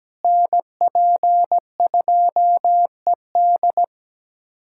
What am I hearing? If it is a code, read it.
Morse code, heard as a keyed tone, about 17 wpm, 700 Hz. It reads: NP2ED
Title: USA Callsign DE Callsign - 5 Characters - 17wpm